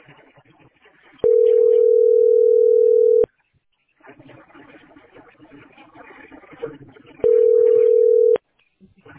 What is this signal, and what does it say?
Ringback tone (North America), 440 + 480 Hz.